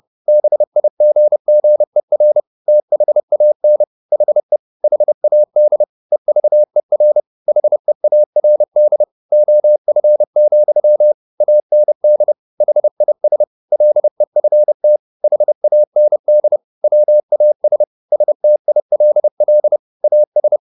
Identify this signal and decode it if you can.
Morse code, heard as a keyed tone, about 30 wpm, 600 Hz. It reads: BIGGER THAN HE HAD EVER HEARD OF, AND HIS LEFT HAND WAS STILL AS
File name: the old man and the sea / the old man and the sea 1 30wpm